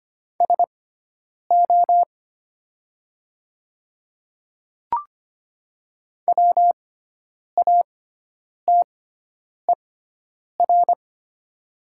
Morse code, heard as a keyed tone, about 25 wpm, spaced as for 9 wpm, 700 Hz. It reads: SO WATER
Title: 6x Character Spacing - Top 100 Words - 25wpm